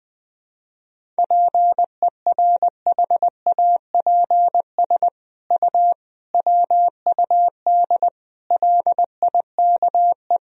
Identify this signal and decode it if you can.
Morse code, heard as a keyed tone, about 20 wpm, 700 Hz. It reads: PERHAPS U WUD LIKE
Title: Sentences from Top 500 Words and Abbreviations 20wpm